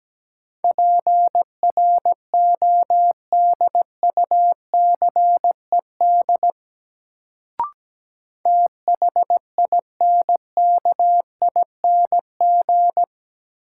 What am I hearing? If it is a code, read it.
Morse code, heard as a keyed tone, about 17 wpm, 700 Hz. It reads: PRODUCED THINKING
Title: Sentences from Top 1000 Words - Review Part 6 - New Word Drill - 17wpm